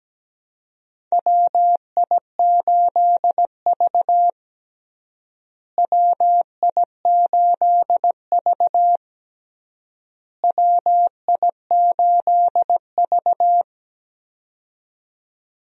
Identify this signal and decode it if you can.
Morse code, heard as a keyed tone, about 17 wpm, 700 Hz. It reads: WI8V WI8V WI8V
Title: USA Call Signs - 4 Characters - Repeated 3x - 17wpm